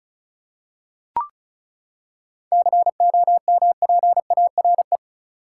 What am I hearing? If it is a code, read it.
Morse code, heard as a keyed tone, about 34 wpm, 700 Hz. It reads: COMPARE
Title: Seven Letter Words - 35wpm